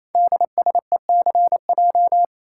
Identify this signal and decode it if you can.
Morse code, heard as a keyed tone, about 27 wpm, 700 Hz. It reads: DSECJ
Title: Code Groups - 5 Letters - 28wpm